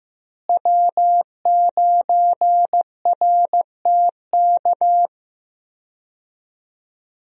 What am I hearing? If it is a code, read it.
Morse code, heard as a keyed tone, about 15 wpm, 700 Hz. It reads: W9RTK
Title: USA Call Signs - 5 Characters - Repeated 2x - 15wpm